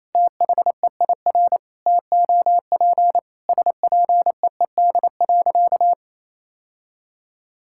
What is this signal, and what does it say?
Morse code, heard as a keyed tone, about 28 wpm, 700 Hz. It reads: THEIR TOP SPEED.
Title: Sentences from Top 1000 Words - Review Part 5 - 28wpm